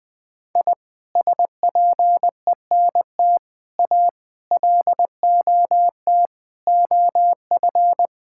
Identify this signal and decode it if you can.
Morse code, heard as a keyed tone, about 20 wpm, 700 Hz. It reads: I SPENT A LOT OF
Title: Sets of 6 Words - 20wpm